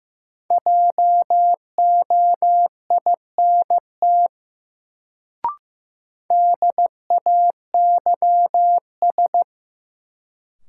Morse code, heard as a keyed tone, about 15 wpm, 700 Hz. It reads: JOINT DAYS
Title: Sets of 5 Words Encore - New Word Drill - 15wpm